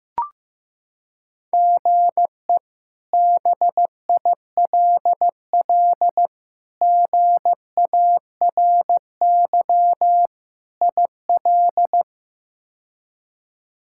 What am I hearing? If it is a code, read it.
Morse code, heard as a keyed tone, about 15 wpm, 700 Hz. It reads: GE BILL GARY IL
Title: K1USN SST QSO Practice - 15wpm